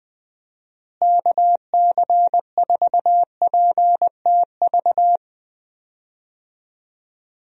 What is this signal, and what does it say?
Morse code, heard as a keyed tone, about 20 wpm, 700 Hz. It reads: KC4PTV
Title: USA Call Signs - 6 Characters - Repeated 2x - 20wpm